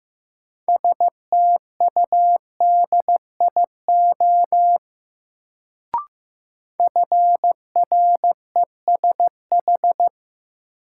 Morse code, heard as a keyed tone, about 15 wpm, 700 Hz. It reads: STUDIO FRESH
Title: Sentences from Top 1200 Words - New Word Drill - 15wpm